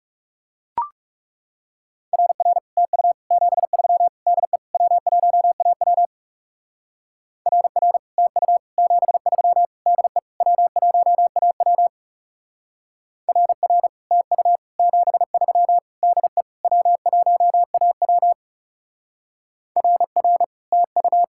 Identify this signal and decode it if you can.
Morse code, heard as a keyed tone, about 36 wpm, 700 Hz. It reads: RR TU 73 DE W1AW RR TU 73 DE W1AW RR TU 73 DE W1AW RR TU
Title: Speed-Racing Common QSO Phrases - 30wpm